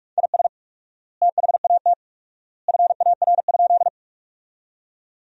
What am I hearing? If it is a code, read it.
Morse code, heard as a keyed tone, about 41 wpm, 700 Hz. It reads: IS THAT FAR?
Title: 4x Word-Spacing - Sentences from Top 300 Words - 45wpm